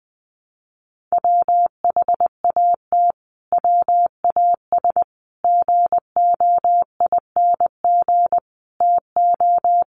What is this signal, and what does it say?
Morse code, heard as a keyed tone, about 20 wpm, 700 Hz. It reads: WHAT WAS GOING TO
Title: Sets of 5 Words Encore 20wpm